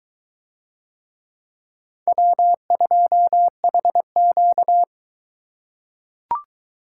Morse code, heard as a keyed tone, about 21 wpm, 700 Hz. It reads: W2HQ